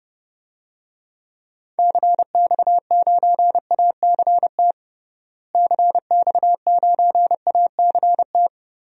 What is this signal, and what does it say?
Morse code, heard as a keyed tone, about 30 wpm, 700 Hz. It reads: CX9ACT CX9ACT
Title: DXCC-100 Call Signs - 6 Characters - Repeated 2x - 30wpm